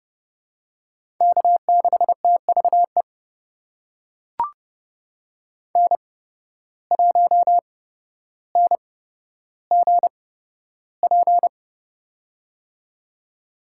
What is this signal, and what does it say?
Morse code, heard as a keyed tone, about 30 wpm, spaced as for 13 wpm, 700 Hz. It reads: K6TVE N1NGP
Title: USA Call Signs - 5 Characters - 8x Character Spacing - 30wpm